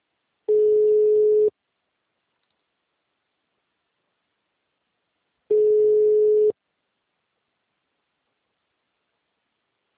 Ringback tone (Europe, 425 Hz).
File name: out- -2011- - - .30814